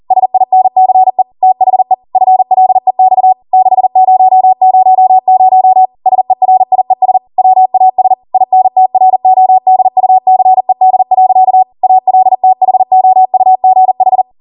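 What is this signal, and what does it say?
Morse code, heard as a keyed tone, about 40 wpm, 750 Hz. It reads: SINCE THE FLEX 6000 SERIES WAS INTRODUCED. ALTHOUGH
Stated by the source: Code practice at 40 wpm.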